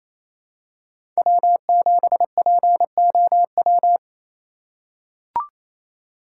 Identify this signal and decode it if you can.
Morse code, heard as a keyed tone, about 28 wpm, 700 Hz. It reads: W7POW
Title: USA Call Signs - 5 Characters - 6x Character Spacing - 28wpm